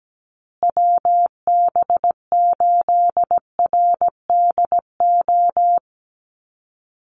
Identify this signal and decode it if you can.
Morse code, heard as a keyed tone, about 17 wpm, 700 Hz. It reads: WB8RDO
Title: USA Call Signs Repeated 2x 17wpm